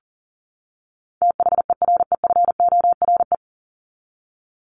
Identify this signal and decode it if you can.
Morse code, heard as a keyed tone, about 40 wpm, 700 Hz. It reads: THEREFORE
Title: Top 501-750 Words 40wpm